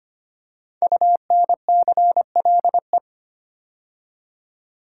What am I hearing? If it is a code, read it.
Morse code, heard as a keyed tone, about 25 wpm, 700 Hz. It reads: UNCLE
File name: 5 Letter Words - First 3 Letters Given - 25wpm